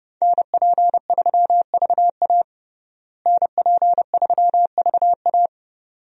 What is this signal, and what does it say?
Morse code, heard as a keyed tone, about 30 wpm, 700 Hz. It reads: NP3VA NP3VA
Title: USA Call Signs - 5 Characters - Repeated 2x - 30wpm